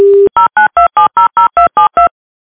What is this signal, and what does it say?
DTMF, heard as telephone touch tones, tones of about 130 ms. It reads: 092700272
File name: modem